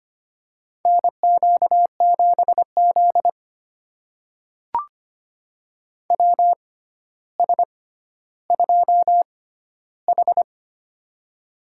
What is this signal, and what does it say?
Morse code, heard as a keyed tone, about 25 wpm, spaced as for 13 wpm, 700 Hz. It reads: NQ7Z WS2H
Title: USA Call Signs - 6x Character Spacing - 25wpm